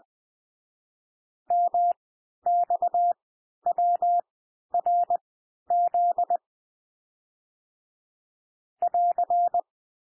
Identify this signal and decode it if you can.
Morse code, heard as a keyed tone, about 20 wpm, spaced as for 11 wpm, 700 Hz. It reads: MXWRZ <AR>
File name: cwmp3 / L1 SOUND LETTERS 20 wpm